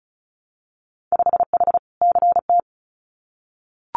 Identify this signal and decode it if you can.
Morse code, heard as a keyed tone, about 36 wpm, 700 Hz. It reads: 5H CT
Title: Winter Field Day Exchanges Encore 35wpm